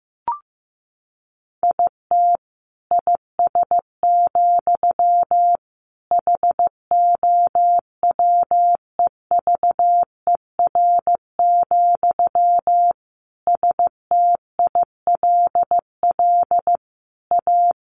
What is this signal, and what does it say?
Morse code, heard as a keyed tone, about 15 wpm, 700 Hz. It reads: IT IS, HOWEVER, STILL A
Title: Sentences from Top 1400 Words 15wpm